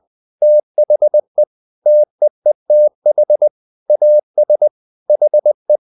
Morse code, heard as a keyed tone, about 20 wpm, 600 Hz. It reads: THE TEETH AS HE
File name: the old man and the sea part 2 20wpm